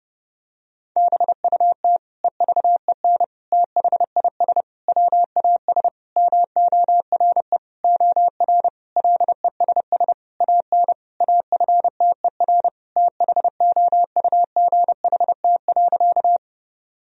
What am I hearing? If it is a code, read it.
Morse code, heard as a keyed tone, about 30 wpm, 700 Hz. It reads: BUT EVEN THIS WAS MORE OR LESS AN AFTER THOUGHT.
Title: Sentences from Top 1000 Words - Review Part 2 - 30wpm